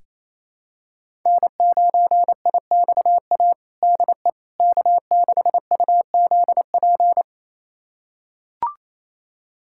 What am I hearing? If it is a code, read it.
Morse code, heard as a keyed tone, about 28 wpm, 700 Hz. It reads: N9IXA DE K6UZP
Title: USA Callsign DE Callsign - 5 Characters - 28wpm